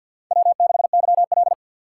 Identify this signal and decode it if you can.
Morse code, heard as a keyed tone, about 54 wpm, 700 Hz. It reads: W6YL
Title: USA Call Signs - 4 Characters - 6x Character Spacing - 50wpm